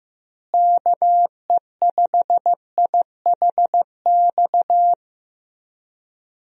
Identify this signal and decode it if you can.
Morse code, heard as a keyed tone, about 15 wpm, 700 Hz. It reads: KE5IHX
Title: USA Call Signs - 6 Characters - Repeated 3x - 15wpm